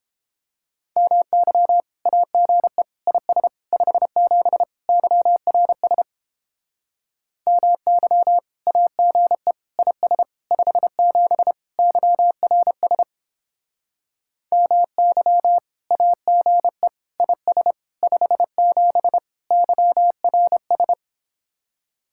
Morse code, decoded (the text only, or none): MY AGE IS 57 YRS MY AGE IS 57 YRS MY AGE IS 57 YRS